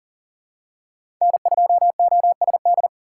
Morse code, heard as a keyed tone, about 39 wpm, 700 Hz. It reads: N2OSD
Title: USA Call Signs - 2x Character Spacing - 40wpm